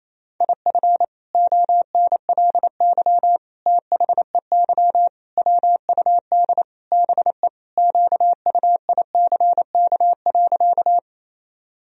Morse code, heard as a keyed tone, about 28 wpm, 700 Hz. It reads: IF ONLY THEY WUD BE QUICK.
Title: Sentences from Top 500 Words and Abbreviations 28wpm